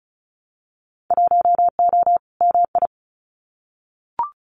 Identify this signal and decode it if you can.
Morse code, heard as a keyed tone, about 36 wpm, 700 Hz. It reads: 1O MI
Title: Winter Field Day Exchanges Encore 35wpm